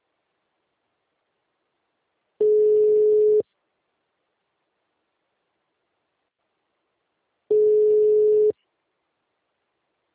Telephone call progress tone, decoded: ringback tone

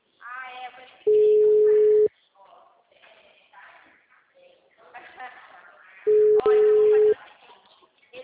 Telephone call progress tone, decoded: ringback tone

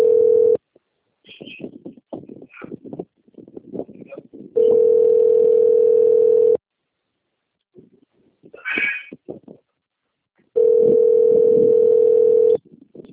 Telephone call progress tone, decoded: ringback tone

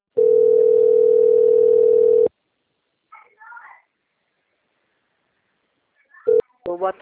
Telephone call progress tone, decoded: ringback tone

